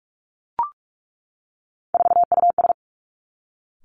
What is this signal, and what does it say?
Morse code, heard as a keyed tone, about 46 wpm, 700 Hz. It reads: 4US
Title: Three Challenging Letter-Number Combinations 45wpm